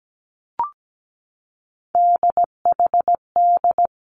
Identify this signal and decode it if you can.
Morse code, heard as a keyed tone, about 17 wpm, 700 Hz. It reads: DHD
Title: Three Challenging Letter-Number Combinations 17wpm